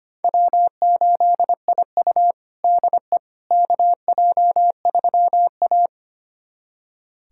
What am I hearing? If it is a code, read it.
Morse code, heard as a keyed tone, about 25 wpm, 700 Hz. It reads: W8IU DE KJ3A
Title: USA Callsign DE Callsign - 4 Characters - 25wpm